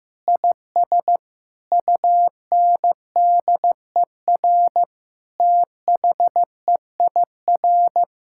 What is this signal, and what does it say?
Morse code, heard as a keyed tone, about 15 wpm, 700 Hz. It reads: IS UNDER THEIR
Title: Sentences from Top 1600 Words 15wpm